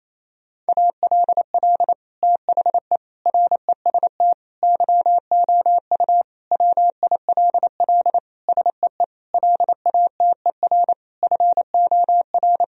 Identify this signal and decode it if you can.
Morse code, heard as a keyed tone, about 28 wpm, 700 Hz. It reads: ALL THE REST YOU WILL SEE LATER FOR
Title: Sentences from Top 1000 Words - Review Part 4 - 28wpm